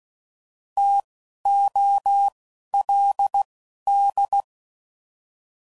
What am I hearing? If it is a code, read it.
Morse code, heard as a keyed tone, about 16 wpm, spaced as for 11 wpm, 800 Hz. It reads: TOLD